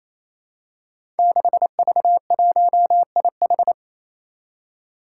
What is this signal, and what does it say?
Morse code, heard as a keyed tone, about 27 wpm, 700 Hz. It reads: 6V1IH
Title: Five Challenging Letter-Number Combinations 28wpm